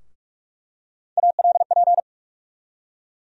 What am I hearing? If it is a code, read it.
Morse code, heard as a keyed tone, about 47 wpm, 700 Hz. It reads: ALP
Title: USA Call Sign Postfixes - 3 Letters 45wpm